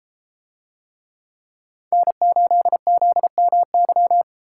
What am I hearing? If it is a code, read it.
Morse code, heard as a keyed tone, about 33 wpm, 700 Hz. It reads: N8ZMY